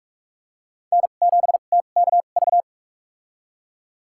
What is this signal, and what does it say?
Morse code, heard as a keyed tone, about 46 wpm, spaced as for 32 wpm, 700 Hz. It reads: N7TKV
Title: USA Call Signs - 5 Characters - 2x Character Spacing - 45wpm